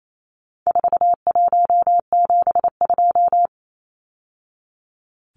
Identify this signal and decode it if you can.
Morse code, heard as a keyed tone, about 28 wpm, 700 Hz. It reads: 4172
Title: Numbers - 4-Digit 28wpm